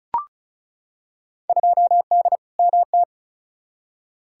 Morse code, heard as a keyed tone, about 36 wpm, 700 Hz. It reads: 2D MT
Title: Field Day Exchanges Encore - 35wpm